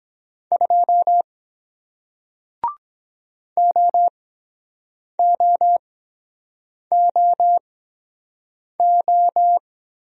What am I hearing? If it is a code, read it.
Morse code, heard as a keyed tone, about 22 wpm, spaced as for 7 wpm, 700 Hz. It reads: 2 OOOO